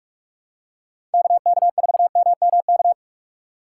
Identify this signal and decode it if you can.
Morse code, heard as a keyed tone, about 42 wpm, 700 Hz. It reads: KK4MMK
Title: USA Call Signs - 6 Characters - 3x Character Spacing - 45wpm